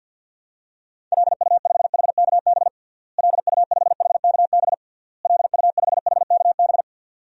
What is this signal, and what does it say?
Morse code, heard as a keyed tone, about 51 wpm, 700 Hz. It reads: LU5HKB LU5HKB LU5HKB
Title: DXCC-100 Call Signs - 6 Characters - Repeated 3x - 50wpm